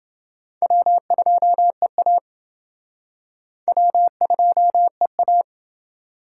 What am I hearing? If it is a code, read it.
Morse code, heard as a keyed tone, about 29 wpm, 700 Hz. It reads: W2EA W2EA